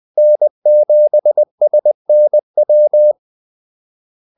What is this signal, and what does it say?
Morse code, heard as a keyed tone, about 20 wpm, 600 Hz. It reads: N7SNW
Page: Kaldesignaler 20 wpm